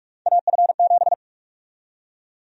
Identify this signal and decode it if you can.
Morse code, heard as a keyed tone, about 46 wpm, 700 Hz. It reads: AF7
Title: USA Call Sign Prefixes - 45wpm